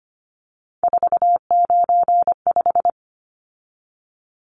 Morse code, heard as a keyed tone, about 25 wpm, 700 Hz. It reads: 495
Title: Numbers - 3-Digit 25wpm